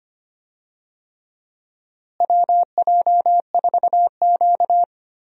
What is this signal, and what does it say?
Morse code, heard as a keyed tone, about 25 wpm, 700 Hz. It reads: WJ4Q
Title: USA Call Signs - 4 Characters - Repeated 3x - 25wpm